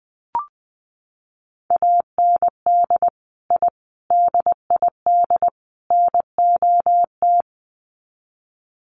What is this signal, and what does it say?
Morse code, heard as a keyed tone, about 20 wpm, 700 Hz. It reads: AND I DID NOT
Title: Sets of 4 Words 20wpm